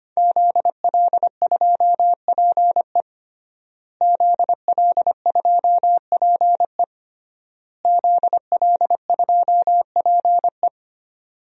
Morse code, heard as a keyed tone, about 25 wpm, 700 Hz. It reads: ZL2PE ZL2PE ZL2PE
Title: DXCC-100 Call Signs - 5 Characters - Repeated 3x - 25wpm